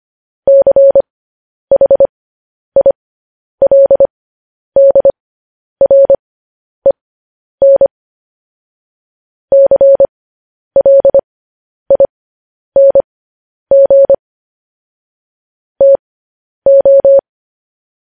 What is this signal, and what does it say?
Morse code, heard as a keyed tone, about 25 wpm, spaced as for 10 wpm, 550 Hz. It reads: CHILDREN CLING TO